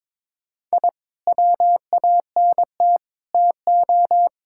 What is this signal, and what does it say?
Morse code, heard as a keyed tone, about 21 wpm, 700 Hz. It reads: I WANT TO
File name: practice / 22wpm / Sentences with 3-Word Phrases - 22wpm